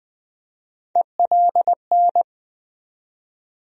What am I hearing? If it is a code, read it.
Morse code, heard as a keyed tone, about 20 wpm, 700 Hz. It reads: ELN
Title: USA Call Sign Postfixes - 3 Letters 20wpm